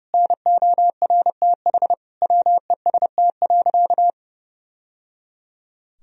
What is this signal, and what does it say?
Morse code, heard as a keyed tone, about 30 wpm, 700 Hz. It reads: NORTH WEST.
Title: Sentences from Top 1000 Words - Review Part 2 - 30wpm